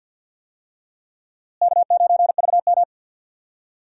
Morse code, heard as a keyed tone, about 45 wpm, 700 Hz. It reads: K9VM
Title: USA Call Signs - 4 Characters - Repeated 3x - 50wpm